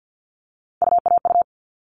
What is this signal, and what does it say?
Morse code, heard as a keyed tone, about 50 wpm, 700 Hz. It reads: UAU
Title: All Cut Numbers - 3-Digit 50wpm